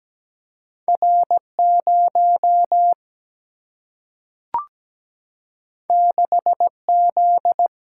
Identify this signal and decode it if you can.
Morse code, heard as a keyed tone, about 17 wpm, 700 Hz. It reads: R 0 6 Z
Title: Single-Letter Single-Number 17wpm